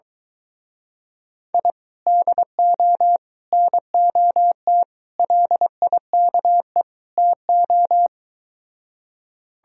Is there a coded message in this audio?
Yes: Morse code from 1.5 to 8.1 s, heard as a keyed tone, about 23 wpm, 700 Hz. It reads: I DO NOT LIKE TO